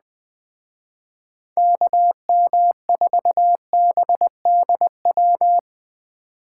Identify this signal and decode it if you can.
Morse code, heard as a keyed tone, about 19 wpm, 700 Hz. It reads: KM4BDW
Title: Speed Racing - USA Call Signs 17wpm